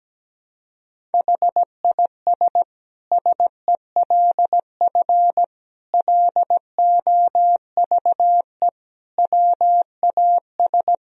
Morse code, heard as a keyed tone, about 17 wpm, 700 Hz. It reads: HIS SELF LOVE WAS
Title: Sentences from Top 1000 Words - Review Part 1 - 17wpm